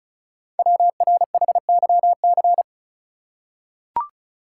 Morse code, heard as a keyed tone, about 36 wpm, 690 Hz. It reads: WRHYC
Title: Code Groups - 5 Letters - 35wpm